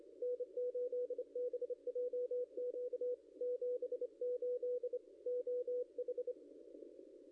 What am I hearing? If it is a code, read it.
Morse code, heard as a keyed tone, about 28 wpm, 490 Hz, with radio noise. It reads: N8BJQ 78 OH
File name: contests / ss-cw-k5tr-op-n5rz -0600